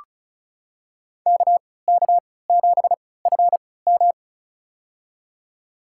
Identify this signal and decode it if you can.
Morse code, heard as a keyed tone, about 36 wpm, spaced as for 20 wpm, 700 Hz. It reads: KK7FM
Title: USA Call Signs - 5 Characters - 3x Character Spacing - 35wpm